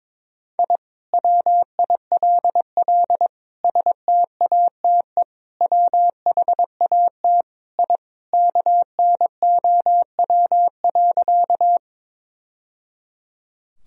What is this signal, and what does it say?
Morse code, heard as a keyed tone, about 21 wpm, 700 Hz. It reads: I WILL STATE WHAT I KNOW.
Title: Sentences from Top 1000 Words - Review Part 2 - 22wpm